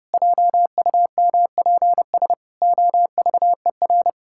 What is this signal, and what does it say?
Morse code, heard as a keyed tone, about 30 wpm, 700 Hz. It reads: JUMPS OVER
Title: Pangram Challenge v2 - 30wpm